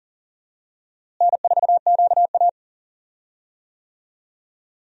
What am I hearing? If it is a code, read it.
Morse code, heard as a keyed tone, about 40 wpm, 700 Hz. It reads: N4QA
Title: USA Call Signs - 4 Characters - Repeated 3x - 40wpm